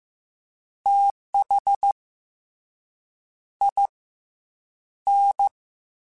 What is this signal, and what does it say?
Morse code, heard as a keyed tone, about 15 wpm, 800 Hz. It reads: TH I N
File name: Misc / Misc-mp3s / Jumps @13